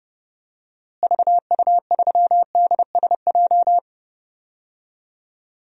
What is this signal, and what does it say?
Morse code, heard as a keyed tone, about 30 wpm, 700 Hz. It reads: VU3DSJ
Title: DXCC-100 Call Signs - 6 Characters - Repeated 3x - 30wpm